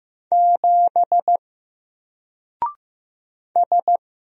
Morse code, heard as a keyed tone, about 15 wpm, 700 Hz. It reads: 7 S
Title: Single Letter-Number Alt 15wpm